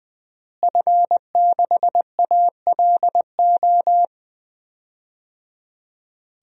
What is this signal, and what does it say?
Morse code, heard as a keyed tone, about 20 wpm, 700 Hz. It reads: F6ALO
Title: DXCC-100 Call Signs - 5 Characters - Repeated 3x - 20wpm